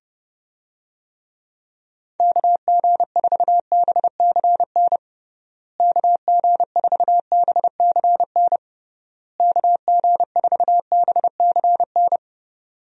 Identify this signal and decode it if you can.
Morse code, heard as a keyed tone, about 30 wpm, 700 Hz. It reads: KG4BCN KG4BCN KG4BCN
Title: USA Call Signs - 6 Characters - Repeated 3x - 30wpm